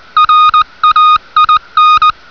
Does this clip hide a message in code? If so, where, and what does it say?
Yes: Morse code from 0.2 to 2.1 s, heard as a keyed tone, about 20 wpm, 1240 Hz. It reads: RAIN